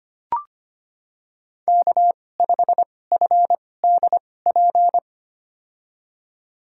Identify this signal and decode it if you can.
Morse code, heard as a keyed tone, about 25 wpm, spaced as for 18 wpm, 700 Hz. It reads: K5FDP
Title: USA Call Signs - 5 Characters - 2x Character Spacing - 25wpm